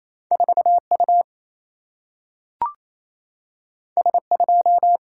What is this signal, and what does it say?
Morse code, heard as a keyed tone, about 28 wpm, 700 Hz. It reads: 4 U S 2
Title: Single-Letter Single-Number 28wpm